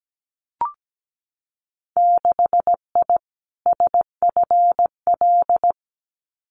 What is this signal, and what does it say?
Morse code, heard as a keyed tone, about 17 wpm, 700 Hz. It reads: 6I SFL
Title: Winter Field Day Exchanges Encore 17wpm